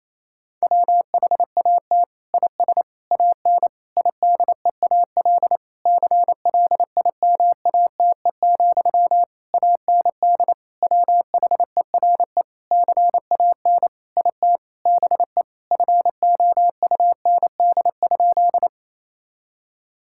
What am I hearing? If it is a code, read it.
Morse code, heard as a keyed tone, about 28 wpm, 700 Hz. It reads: WHAT IS AN IDEAL CLIMATE, AND WHERE CAN IT BE FOUND?
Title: Sentences from Top 1600 Words 28wpm